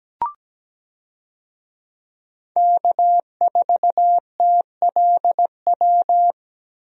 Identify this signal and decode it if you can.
Morse code, heard as a keyed tone, about 17 wpm, 700 Hz. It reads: K4TLW
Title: USA Call Signs - 5 Characters - Repeated 2x - 17wpm